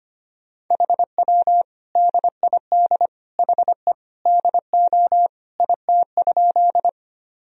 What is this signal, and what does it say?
Morse code, heard as a keyed tone, about 25 wpm, 700 Hz. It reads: HW DID HE DO IT?
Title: Sentences from Top 100 Words and Abbreviations 25wpm